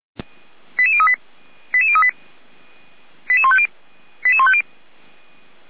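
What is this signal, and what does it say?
Five-tone selective call (ZVEI), heard as quick runs of five tones, tones of about 70 ms. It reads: 88038 88038 88150 88150
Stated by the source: SDR-received ZVEI five-tone call 88425.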